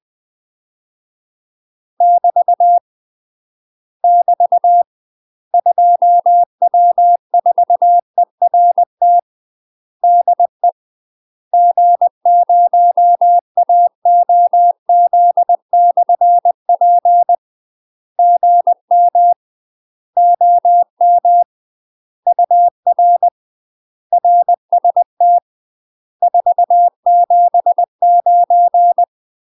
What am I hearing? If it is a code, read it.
Morse code, heard as a keyed tone, about 20 wpm, 700 Hz. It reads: <BT> <BT> 2W4ERT DE G0AOZ/P GM OM UR RST 479
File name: UK MASTER QSOs TRAINING / 20WPM / UK MASTER QSOs 0003